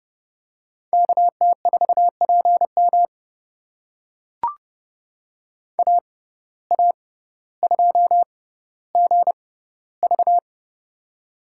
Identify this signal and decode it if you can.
Morse code, heard as a keyed tone, about 31 wpm, spaced as for 16 wpm, 700 Hz. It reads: KT4PM AA2GV
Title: USA Call Signs - 5 Characters - 6x Character Spacing - 30wpm